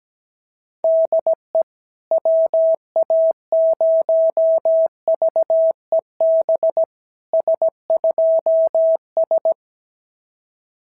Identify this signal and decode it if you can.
Morse code, heard as a keyed tone, about 17 wpm, 690 Hz. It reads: DE WA0VEB S2S
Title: SOTA QSO Practice 17wpm